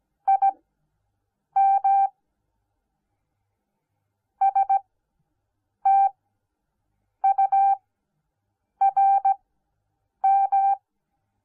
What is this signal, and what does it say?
Morse code, heard as a keyed tone, about 17 wpm, spaced as for 7 wpm, 780 Hz. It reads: IM STURM